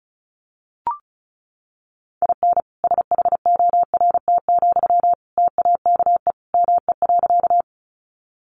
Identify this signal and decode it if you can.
Morse code, heard as a keyed tone, about 34 wpm, 700 Hz. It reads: IN SHORT, TAKE ME.
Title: Sentences from Top 400 Words 35wpm